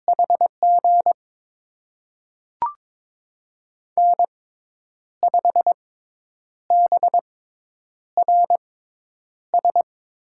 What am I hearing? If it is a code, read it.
Morse code, heard as a keyed tone, about 22 wpm, spaced as for 9 wpm, 700 Hz. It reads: HG N5BRS
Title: USA Call Signs - 6x Character Spacing - 22wpm